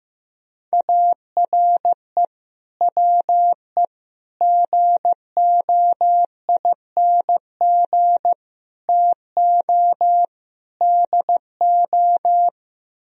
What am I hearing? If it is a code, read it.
Morse code, heard as a keyed tone, about 15 wpm, 700 Hz. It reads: ARE WE GOING TO DO
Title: Speed-Racing - Sets of 5 Words 15wpm